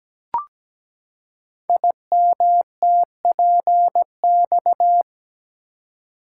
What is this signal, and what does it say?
Morse code, heard as a keyed tone, about 17 wpm, 700 Hz. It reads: IMTPX
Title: Code Groups - 5 Letters - 17wpm